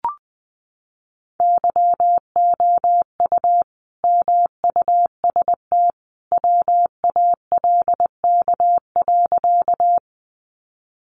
Morse code, heard as a keyed tone, about 20 wpm, 700 Hz. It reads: YOU MUST WALK.
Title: Sentences from Top 300 Words 20wpm